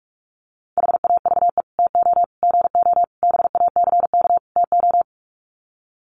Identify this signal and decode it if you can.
Morse code, heard as a keyed tone, about 44 wpm, 700 Hz. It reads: HAVE TO GO BACK TO
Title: Sets of 5 Words Encore2 45wpm